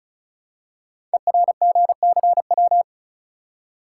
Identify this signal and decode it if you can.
Morse code, heard as a keyed tone, about 36 wpm, 690 Hz. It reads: ERGCW
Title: Code Groups - 5 Letter:Numbers - 35wpm